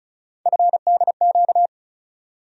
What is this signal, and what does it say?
Morse code, heard as a keyed tone, about 34 wpm, 700 Hz. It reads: FDQ
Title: USA Call Sign Postfixes - 3 Letters 35wpm